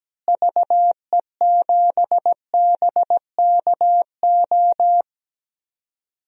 Morse code, heard as a keyed tone, about 17 wpm, 700 Hz. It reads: VE7BKO
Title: DXCC-100 Call Signs - 6 Characters - Repeated 3x - 17wpm